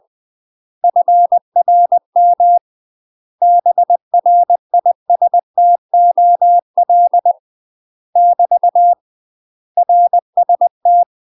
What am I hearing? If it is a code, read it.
Morse code, heard as a keyed tone, about 20 wpm, 700 Hz. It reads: FRM BRISTOL <BT> RST
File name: UK MASTER QSOs TRAINING/20WPM